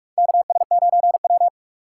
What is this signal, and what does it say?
Morse code, heard as a keyed tone, about 46 wpm, 700 Hz. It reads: KS9W
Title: K1USN SST QSO Practice - 45wpm